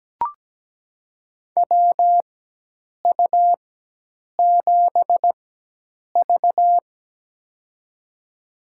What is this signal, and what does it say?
Morse code, heard as a keyed tone, about 17 wpm, spaced as for 8 wpm, 700 Hz. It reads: WU7V
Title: USA Call Signs - 4 Characters - 4x Character Spacing - 17wpm